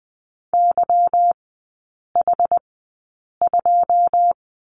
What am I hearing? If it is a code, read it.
Morse code, heard as a keyed tone, about 20 wpm, spaced as for 8 wpm, 700 Hz. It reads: YH2
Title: 145 Lesson 24e - Letter y - Three Characters 20wpm